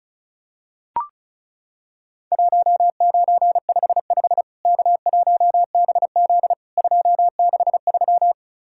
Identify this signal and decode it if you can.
Morse code, heard as a keyed tone, about 35 wpm, 700 Hz. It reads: 1955 K1BZ 263
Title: Worked All Europe DX Contest - QTC Exchange - 35wpm